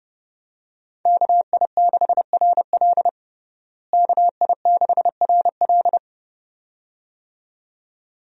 Morse code, heard as a keyed tone, about 30 wpm, 700 Hz. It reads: KI6RL KI6RL
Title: USA Call Signs - 5 Characters - Repeated 2x - 30wpm